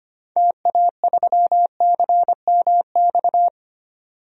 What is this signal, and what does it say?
Morse code, heard as a keyed tone, about 25 wpm, 700 Hz. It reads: TA3CMX
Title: DXCC-100 Call Signs - 6 Characters - 25wpm